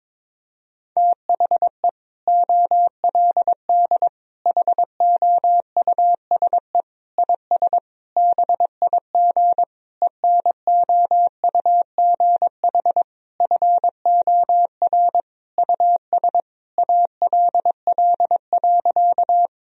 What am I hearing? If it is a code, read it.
Morse code, heard as a keyed tone, about 22 wpm, 700 Hz. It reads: THE OLD HOUSE IS BIG ENOUGH FOR US ALL.
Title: Sentences from Top 1000 Words - Review Part 1 - 22wpm